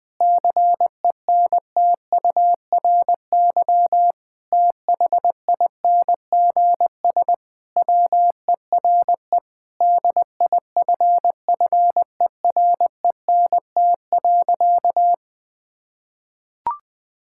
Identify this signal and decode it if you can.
Morse code, heard as a keyed tone, about 20 wpm, 700 Hz. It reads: CENTURY THINGS WERE DIFFERENT.
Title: Sentences with 6-Word Phrases - 20wpm